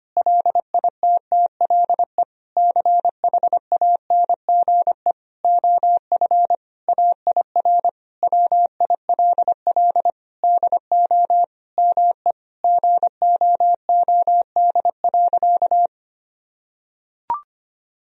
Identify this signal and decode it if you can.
Morse code, heard as a keyed tone, about 25 wpm, 700 Hz. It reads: LITTLE CHANGE OF AIR WILL DO ME GOOD.
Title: Sentences from Top 1000 Words - Review Part 1 - 25wpm